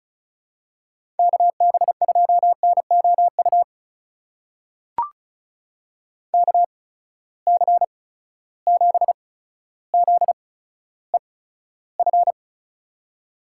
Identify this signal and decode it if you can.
Morse code, heard as a keyed tone, about 36 wpm, spaced as for 15 wpm, 700 Hz. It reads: KB2NOU KC7ZEF
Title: USA Call Signs - 8x Character Spacing - 35wpm